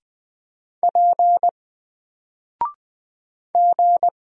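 Morse code, heard as a keyed tone, about 20 wpm, 700 Hz. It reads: P G
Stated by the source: Single letter and number Morse drill at 20 wpm, from N4LCD.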